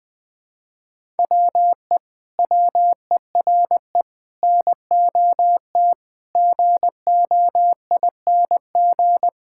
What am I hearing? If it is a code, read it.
Morse code, heard as a keyed tone, about 21 wpm, 700 Hz. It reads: WE WERE NOT GOING
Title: Sentences with 7-Word Phrases - 20wpm